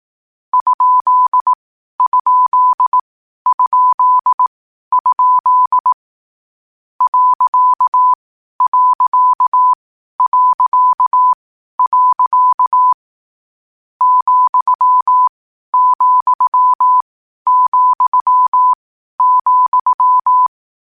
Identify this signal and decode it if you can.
Morse code, heard as a keyed tone, about 18 wpm, spaced as for 12 wpm, 1000 Hz. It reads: ???? .... ,,,,
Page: Hier findet Ihr das Morsealphabet in langsamer Geschwindigkeit zum Hören. Dabei ist jeder Buchstabe vier Mal hintereinander aufgenommen worden, danach kommt direkt der nächste.